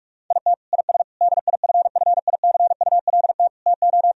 Morse code, heard as a keyed tone, about 43 wpm, 700 Hz. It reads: IT IS DIFFICULT TO
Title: Sentences from Top 1000 Words - Review Part 3 - 45wpm